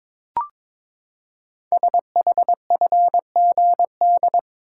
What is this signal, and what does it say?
Morse code, heard as a keyed tone, about 22 wpm, 700 Hz. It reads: SHFGD
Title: Code Groups - 5 Letters - 22wpm